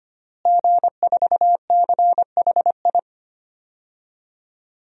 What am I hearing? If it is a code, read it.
Morse code, heard as a keyed tone, about 25 wpm, 700 Hz. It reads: G4CHI
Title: DXCC-100 Call Signs - 5 Characters - Repeated 2x - 25wpm